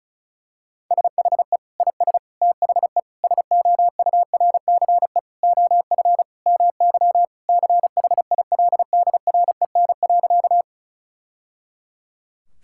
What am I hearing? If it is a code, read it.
Morse code, heard as a keyed tone, about 36 wpm, 700 Hz. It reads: SHE IS THE SOURCE OF MY CHILDREN.
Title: Sentences from Top 1000 Words - Review Part 5 - 35wpm